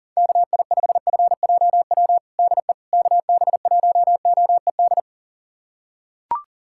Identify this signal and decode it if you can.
Morse code, heard as a keyed tone, about 40 wpm, 700 Hz. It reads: KI5FJW DE KB1OED
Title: USA Callsign DE Callsign - 6 Characters - 40wpm